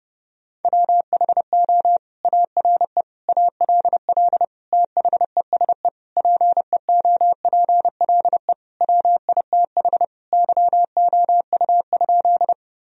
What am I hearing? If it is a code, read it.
Morse code, heard as a keyed tone, about 32 wpm, 700 Hz. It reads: WHO ARE ALL THESE PEOPLE WITH YOU?
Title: Sentences from Top 1000 Words - Review Part 1 - 30wpm